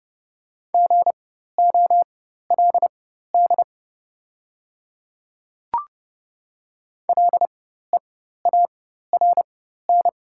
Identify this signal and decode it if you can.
Morse code, heard as a keyed tone, about 30 wpm, spaced as for 14 wpm, 690 Hz. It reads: GOLD LEARN
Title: 4x Character Spacing - Top 500 Words - 30wpm